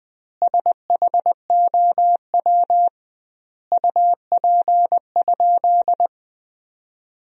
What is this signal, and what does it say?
Morse code, heard as a keyed tone, about 21 wpm, 700 Hz. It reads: SHOW UP?
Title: 2x Word-Spacing - Sentences from Top 200 Words - 20wpm